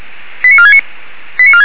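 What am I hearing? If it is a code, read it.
Five-tone selective call (ZVEI), heard as quick runs of five tones, tones of about 80 ms. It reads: 89469
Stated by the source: Five-tone ZVEI selective call received on an SDR.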